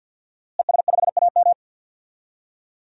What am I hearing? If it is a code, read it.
Morse code, heard as a keyed tone, about 53 wpm, 700 Hz. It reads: ES5AM
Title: DXCC-100 Call Signs - 5 Characters - Repeated 3x - 50wpm